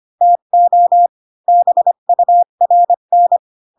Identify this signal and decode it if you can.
Morse code, heard as a keyed tone, about 25 wpm, 700 Hz. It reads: TO BURN
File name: Pinocchio / Pinocchio ch1 20 25 ik1hge